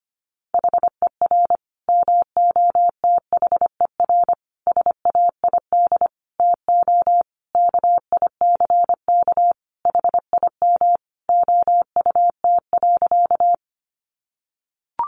Morse code, heard as a keyed tone, about 25 wpm, 700 Hz. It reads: HER MOTHER SAID TO KICK HIM OUT.
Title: Sentences from Top 1300 Words 25wpm